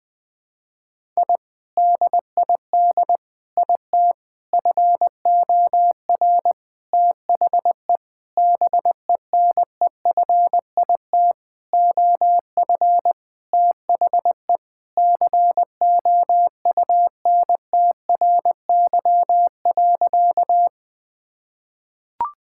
Morse code, heard as a keyed tone, about 20 wpm, 700 Hz. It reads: I DID IT FOR THE BENEFIT OF THE COUNTRY.
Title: Sentences from Top 1000 Words - Review Part 6 - 20wpm